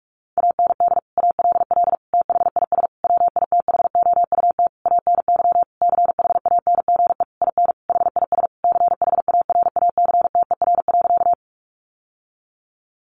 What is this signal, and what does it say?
Morse code, heard as a keyed tone, about 44 wpm, 700 Hz. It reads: AND ALL THIS WITHOUT ANY CHANGE IN HIS CHARACTER.
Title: Sentences from Top 1200 Words 45wpm